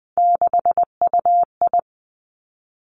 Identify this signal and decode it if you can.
Morse code, heard as a keyed tone, about 20 wpm, 700 Hz. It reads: 6UI
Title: Three Challenging Letter-Number Combinations 20wpm